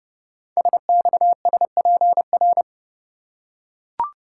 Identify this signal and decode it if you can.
Morse code, heard as a keyed tone, about 30 wpm, 700 Hz. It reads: SXSPR
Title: Code Groups - 5 Letters - 30wpm